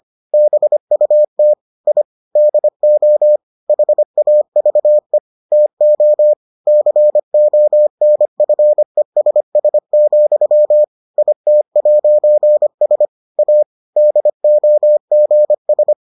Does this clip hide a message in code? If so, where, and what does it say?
Yes: Morse code from 0.3 to 15.9 s, heard as a keyed tone, about 25 wpm, 600 Hz. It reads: BUT I DO HAVE TO CONFESS, IT'S A DOGS